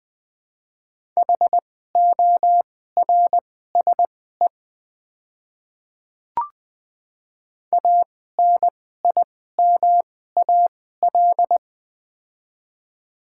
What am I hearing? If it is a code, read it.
Morse code, heard as a keyed tone, about 20 wpm, spaced as for 14 wpm, 700 Hz. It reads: HORSE ANIMAL
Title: 2x Character Spacing - Top 300 Words - 20wpm